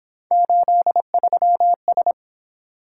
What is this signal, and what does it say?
Morse code, heard as a keyed tone, about 26 wpm, 700 Hz. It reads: 83S
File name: Speed-Racing - Three Letter-Number Combinations - 22wpm